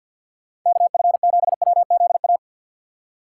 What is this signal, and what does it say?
Morse code, heard as a keyed tone, about 53 wpm, 700 Hz. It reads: KF7WZA
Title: USA Call Signs - 6 Characters - 8x Character Spacing - 50wpm